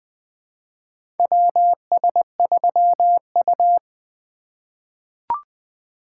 Morse code, heard as a keyed tone, about 20 wpm, 700 Hz. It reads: WS3U
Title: USA Call Signs - 4 Characters - 2x Character Spacing - 20wpm